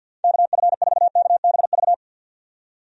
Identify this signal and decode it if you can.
Morse code, heard as a keyed tone, about 51 wpm, 700 Hz. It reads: KF4KBV
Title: USA Call Signs - 6 Characters - 50wpm